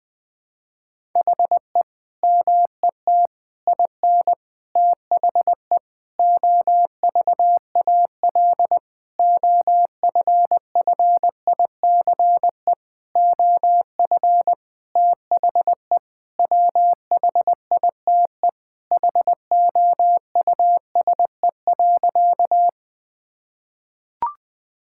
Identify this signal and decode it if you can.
Morse code, heard as a keyed tone, about 20 wpm, 700 Hz. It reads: HE MET IN THE OVAL OFFICE OF THE WHITE HOUSE.
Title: Sentences with 8-Word Phrases - 20wpm